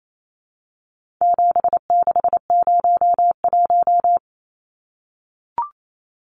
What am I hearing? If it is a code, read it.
Morse code, heard as a keyed tone, about 28 wpm, 700 Hz. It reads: 7601
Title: Numbers - 4-Digit 28wpm